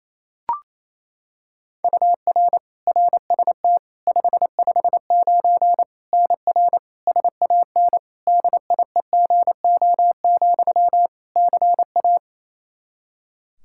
Morse code, heard as a keyed tone, about 28 wpm, 700 Hz. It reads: UR RST 559 NR SAN DIEGO, CA
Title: Practice QSO - Exchange 1 - Beginning - 28wpm